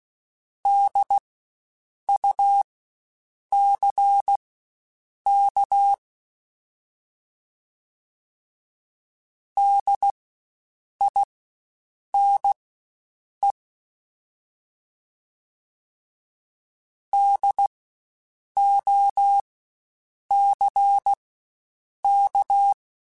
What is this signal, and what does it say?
Morse code, heard as a keyed tone, about 16 wpm, spaced as for 7 wpm, 800 Hz. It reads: DUCK DINE DOCK
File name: Spaced Alphas / Spaced Alphas mp3s / Ds sp @15